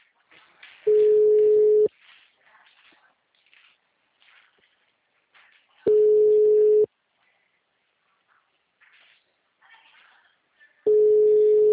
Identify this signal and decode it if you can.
Ringback tone (Europe, 425 Hz).